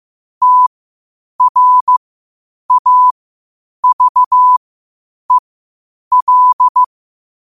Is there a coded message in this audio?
Yes: Morse code from 0.4 to 6.8 s, heard as a keyed tone, about 15 wpm, spaced as for 8 wpm, 1000 Hz. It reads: TRAVEL